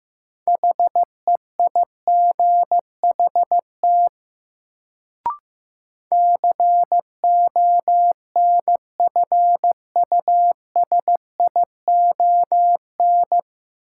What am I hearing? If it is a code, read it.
Morse code, heard as a keyed tone, about 15 wpm, 700 Hz. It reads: HEIGHT CONFUSION
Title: Sentences from Top 1600 Words - New Word Drill - 15wpm